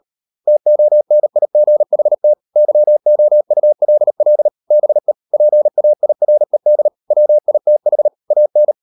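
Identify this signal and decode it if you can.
Morse code, heard as a keyed tone, about 38 wpm, 600 Hz. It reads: TONIGHT YOULL BE PAIRED WITH AN
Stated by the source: Morse code practice at 38 WPM, sending Ken Follett's novel Never in English, from CW med Gnister.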